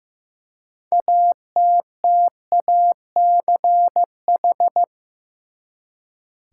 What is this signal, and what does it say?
Morse code, heard as a keyed tone, about 15 wpm, 700 Hz. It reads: ATTACH
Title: Six Letter Words - 15wpm